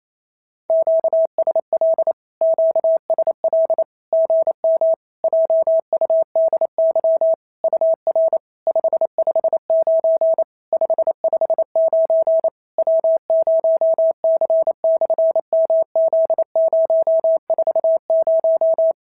Morse code, heard as a keyed tone, about 28 wpm, 650 Hz. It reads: QSL QSL GM JUDY UR 559 559 W0C/MZ040
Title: SOTA QSO Practice 28wpm